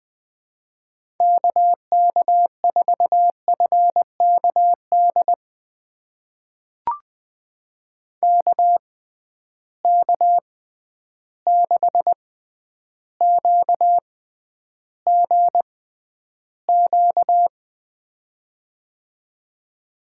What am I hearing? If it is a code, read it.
Morse code, heard as a keyed tone, about 21 wpm, spaced as for 10 wpm, 700 Hz. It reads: KK4FKD KK6QGQ
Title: USA Call Signs - 6 Characters - 6x Character Spacing - 20wpm